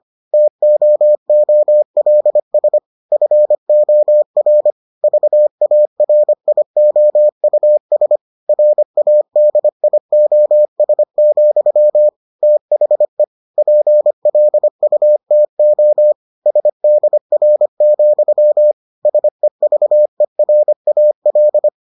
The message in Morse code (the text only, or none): TOOLS FOR VARIOUS RADIOS, THE PLUTO SDR, SEVERAL